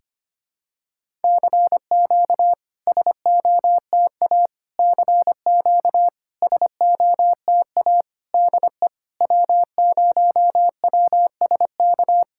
Morse code, heard as a keyed tone, about 25 wpm, 690 Hz. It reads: CQ SOTA CQ SOTA DE W0WSK
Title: SOTA QSO Practice 25wpm